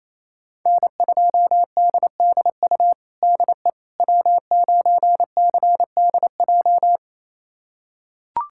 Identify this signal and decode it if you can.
Morse code, heard as a keyed tone, about 28 wpm, 700 Hz. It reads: N2DDU DE W9CDJ
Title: USA Callsign DE Callsign - 5 Characters - 28wpm